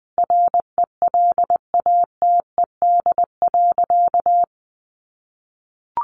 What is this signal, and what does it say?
Morse code, heard as a keyed tone, about 20 wpm, 700 Hz. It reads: RELATED.
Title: Sentences from Top 1400 Words 20wpm